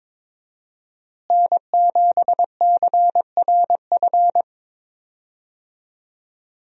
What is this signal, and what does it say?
Morse code, heard as a keyed tone, about 22 wpm, 700 Hz. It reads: N7CRF
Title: USA Call Signs - 5 Characters - Repeated 3x - 22wpm